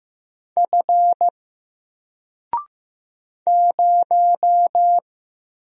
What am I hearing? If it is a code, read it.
Morse code, heard as a keyed tone, about 15 wpm, 700 Hz. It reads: F 0
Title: Single Character (Letter:Number:Punctuation:Prosign) Alt 15wpm